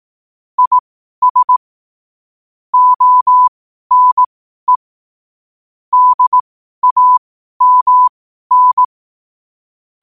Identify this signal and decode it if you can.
Morse code, heard as a keyed tone, about 18 wpm, spaced as for 12 wpm, 1000 Hz. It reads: IS ONE DAMN